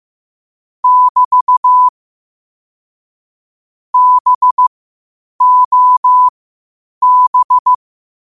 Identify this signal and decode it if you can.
Morse code, heard as a keyed tone, about 15 wpm, spaced as for 8 wpm, 1000 Hz. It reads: <BT> BOB
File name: qotdfiles / 8wpm